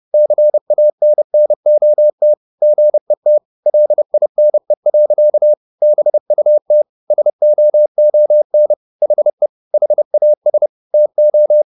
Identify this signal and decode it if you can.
Morse code, heard as a keyed tone, about 30 wpm, 600 Hz. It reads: CANNOT GET LINE. BUT SOON HE HAS TO